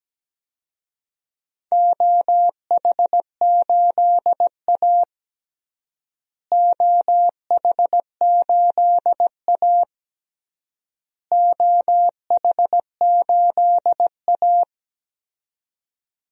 Morse code, heard as a keyed tone, about 17 wpm, 700 Hz. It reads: OH8A OH8A OH8A
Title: DXCC-100 Call Signs - 4 Characters - Repeated 3x - 17wpm